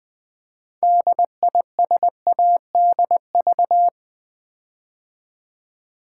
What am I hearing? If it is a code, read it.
Morse code, heard as a keyed tone, about 20 wpm, 700 Hz. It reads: DISADV
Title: Build A Word - 6 Letters At A Time - 12-Letter Words - 20wpm